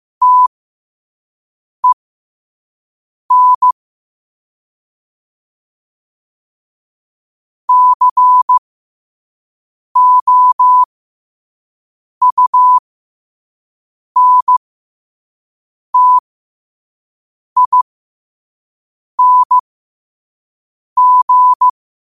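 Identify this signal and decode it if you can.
Morse code, heard as a keyed tone, about 15 wpm, spaced as for 5 wpm, 1000 Hz. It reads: TEN COUNTING
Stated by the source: Morse code sent at 5 wpm for learning.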